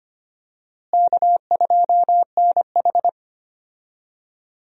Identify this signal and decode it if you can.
Morse code, heard as a keyed tone, about 25 wpm, 700 Hz. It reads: K2NH
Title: USA Call Signs - 4 Characters - Repeated 3x - 25wpm